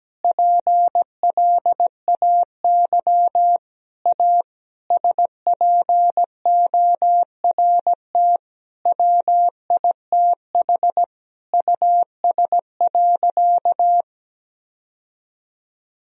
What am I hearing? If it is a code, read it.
Morse code, heard as a keyed tone, about 17 wpm, 700 Hz. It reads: PLAY A SPORT WITH US.
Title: Sentences from Top 1000 Words - Review Part 4 - 17wpm